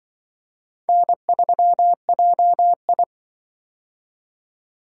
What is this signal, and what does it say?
Morse code, heard as a keyed tone, about 24 wpm, 700 Hz. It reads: N3JI
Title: Speed-Racing - Four Letter-Number Combinations - 20wpm